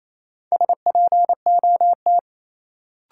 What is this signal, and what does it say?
Morse code, heard as a keyed tone, about 28 wpm, 700 Hz. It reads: SPOT
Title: Sentences from Top 1100 Words - New Word Drill Alt - 28wpm